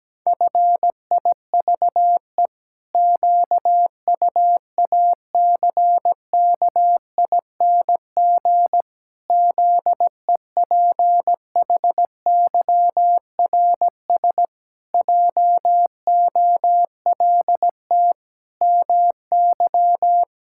Morse code, heard as a keyed tone, about 17 wpm, 700 Hz. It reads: FIVE QUACKING ZEPHYRS JOLT MY
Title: Pangram Challenge v2 - 17wpm